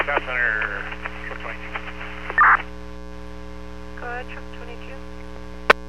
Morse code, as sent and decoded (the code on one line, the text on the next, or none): none